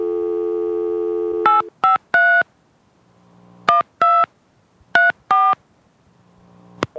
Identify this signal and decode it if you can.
DTMF, heard as telephone touch tones, tones of about 170 ms. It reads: 853 12 34
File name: touchtone phone